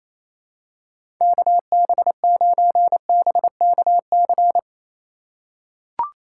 Morse code, heard as a keyed tone, about 28 wpm, 700 Hz. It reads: KB9BKC
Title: USA Call Signs - 6 Characters - 2x Character Spacing - 28wpm